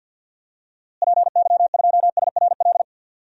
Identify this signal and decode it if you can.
Morse code, heard as a keyed tone, about 52 wpm, 700 Hz. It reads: PY2SRL
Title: DXCC-100 Call Signs - 6 Characters - Repeated 2x - 50wpm